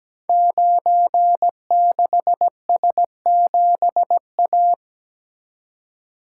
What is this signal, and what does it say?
Morse code, heard as a keyed tone, about 17 wpm, 700 Hz. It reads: 96S7A
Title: Code Groups - 5 Letter:Numbers - 17wpm